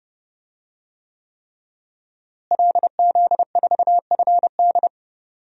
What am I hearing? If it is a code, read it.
Morse code, heard as a keyed tone, about 30 wpm, 700 Hz. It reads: LZ4FD
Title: DXCC-100 Call Signs - 5 Characters - Repeated 3x - 30wpm